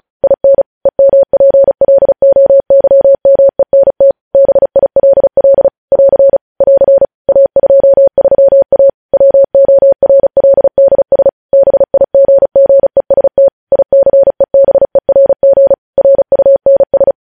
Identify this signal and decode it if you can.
Morse code, heard as a keyed tone, about 36 wpm, 550 Hz. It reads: IN EMPLOYMENT BILL <AR> <AR> A23A WORLDS BIGGEST ICEBERG RUNS
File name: podcast / News headlines-1x-35x35